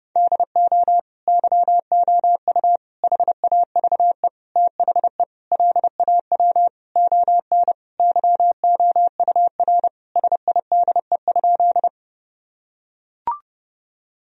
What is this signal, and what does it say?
Morse code, heard as a keyed tone, about 30 wpm, 700 Hz. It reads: DO YOU HAVE THE LAW ON YOUR SIDE?
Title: Sentences from Top 1000 Words - Review Part 3 - 30wpm